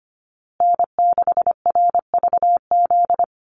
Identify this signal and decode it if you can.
Morse code, heard as a keyed tone, about 25 wpm, 700 Hz. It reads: N6RVZ
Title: USA Call Signs Repeated 3x 25wpm